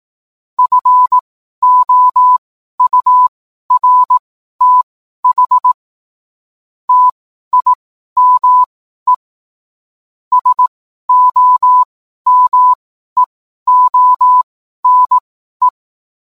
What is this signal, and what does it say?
Morse code, heard as a keyed tone, about 18 wpm, spaced as for 12 wpm, 1000 Hz. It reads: FOURTH TIME SOMEONE